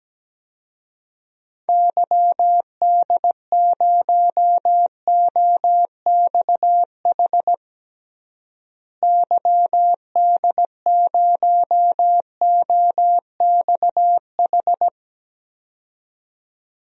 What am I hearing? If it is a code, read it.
Morse code, heard as a keyed tone, about 17 wpm, 700 Hz. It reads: YD0OXH YD0OXH
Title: DXCC-100 Call Signs - 6 Characters - Repeated 2x - 17wpm